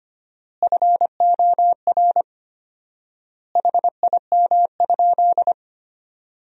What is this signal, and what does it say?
Morse code, heard as a keyed tone, about 25 wpm, 700 Hz. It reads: FOR HIM?
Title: 4x Word-Spacing - Sentences from Top 400 Words - 25wpm